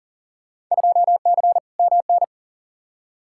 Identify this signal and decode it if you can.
Morse code, heard as a keyed tone, about 40 wpm, 700 Hz. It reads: 2C MN
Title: Field Day Exchanges Encore - 40wpm